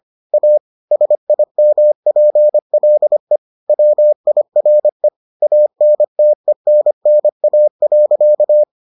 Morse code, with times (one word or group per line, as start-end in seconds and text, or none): 0.3-0.6 A
0.9-3.4 SIMPLE
3.7-5.1 WIRE
5.4-8.6 ANTENNA.